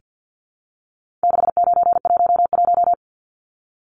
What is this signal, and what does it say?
Morse code, heard as a keyed tone, about 48 wpm, 700 Hz. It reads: 6911
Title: Numbers - 4-Digit 50wpm